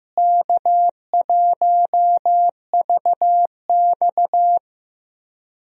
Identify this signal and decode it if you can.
Morse code, heard as a keyed tone, about 15 wpm, 700 Hz. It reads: K1VX
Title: USA Call Signs - 4 Characters - Repeated 3x - 15wpm